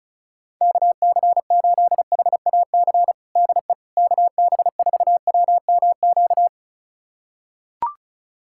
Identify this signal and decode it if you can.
Morse code, heard as a keyed tone, about 34 wpm, 700 Hz. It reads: KC8HAC DE KB4WMQ
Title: USA Callsign DE Callsign - 6 Characters - 35wpm